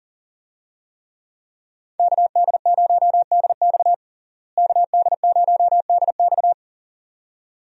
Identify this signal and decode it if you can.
Morse code, heard as a keyed tone, about 40 wpm, 700 Hz. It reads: KD0DX KD0DX
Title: USA Call Signs - 5 Characters - Repeated 2x - 40wpm